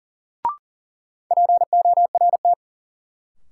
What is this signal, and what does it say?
Morse code, heard as a keyed tone, about 40 wpm, 700 Hz. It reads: PORT
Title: Sentences from Top 1000 Words - Review Part 1 - New Word Drill - 40wpm